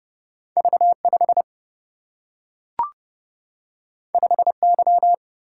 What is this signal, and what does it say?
Morse code, heard as a keyed tone, about 30 wpm, 700 Hz. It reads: V5 5Y
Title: Single-Letter Single-Number 30wpm